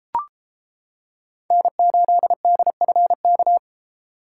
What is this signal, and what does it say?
Morse code, heard as a keyed tone, about 28 wpm, 700 Hz. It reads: N8DFK